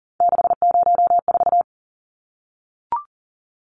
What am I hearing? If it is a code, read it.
Morse code, heard as a keyed tone, about 40 wpm, 700 Hz. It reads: 604
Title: Numbers - 3-Digit 40wpm